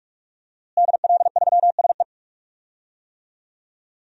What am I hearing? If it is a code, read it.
Morse code, heard as a keyed tone, about 46 wpm, 700 Hz. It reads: DL3SE
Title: DXCC-100 Call Signs - 5 Characters - Repeated 2x - 45wpm